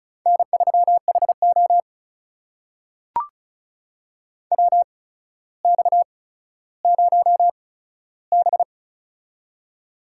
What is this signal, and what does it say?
Morse code, heard as a keyed tone, about 36 wpm, spaced as for 15 wpm, 700 Hz. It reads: N3HO WX0B
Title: USA Call Signs - 4 Characters - 8x Character Spacing - 35wpm